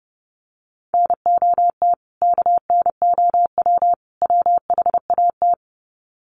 Morse code, heard as a keyed tone, about 30 wpm, 700 Hz. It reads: NOT KNOW WHAT
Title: Sets of 3 Words 30wpm